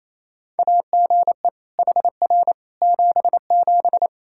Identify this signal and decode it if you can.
Morse code, heard as a keyed tone, about 28 wpm, 700 Hz. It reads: AGE HR 77
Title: Practice QSO - Exchange 3 - More Info - 28wpm